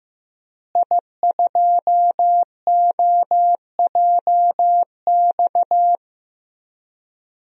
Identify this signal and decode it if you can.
Morse code, heard as a keyed tone, about 15 wpm, 700 Hz. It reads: I2OJX
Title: DXCC-100 Call Signs - 5 Characters - Repeated 3x - 15wpm